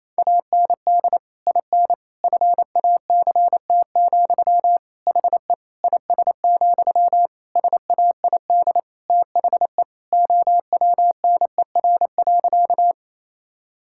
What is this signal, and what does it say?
Morse code, heard as a keyed tone, about 28 wpm, 700 Hz. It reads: AND IN FACT, HE IS, SAID THE OWNER.
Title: Sentences from Top 1600 Words 28wpm